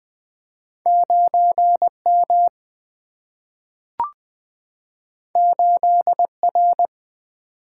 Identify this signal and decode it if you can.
Morse code, heard as a keyed tone, about 20 wpm, 700 Hz. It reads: 9 M 8 R
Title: Single-Letter Single-Number 20wpm